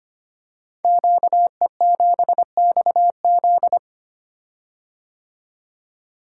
Morse code, heard as a keyed tone, about 25 wpm, 700 Hz. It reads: QE7XZ
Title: Code Groups - 5 Letter:Numbers - 25wpm